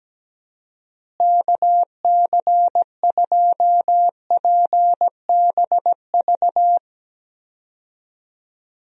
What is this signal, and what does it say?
Morse code, heard as a keyed tone, about 17 wpm, 700 Hz. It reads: KC2PBV
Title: USA Call Signs - 6 Characters - Repeated 3x - 17wpm